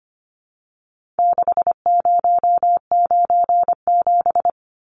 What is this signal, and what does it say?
Morse code, heard as a keyed tone, about 25 wpm, 700 Hz. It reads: 6097
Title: Numbers - 4-Digit 25wpm